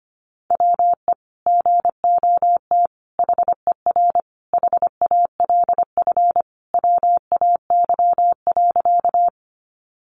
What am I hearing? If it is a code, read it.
Morse code, heard as a keyed tone, about 25 wpm, 700 Hz. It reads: WE GOT HER HALF WAY.
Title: Sentences from Top 400 Words 25wpm